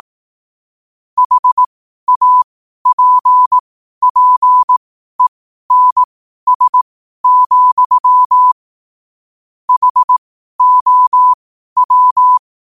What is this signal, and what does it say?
Morse code, heard as a keyed tone, about 18 wpm, spaced as for 12 wpm, 1000 Hz. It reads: HAPPENS, HOW
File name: qotdfiles / 12wpm